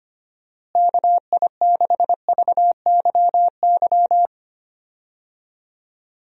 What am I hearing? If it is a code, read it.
Morse code, heard as a keyed tone, about 25 wpm, 700 Hz. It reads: KI6VYY
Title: USA Call Signs - 6 Characters - Repeated 3x - 25wpm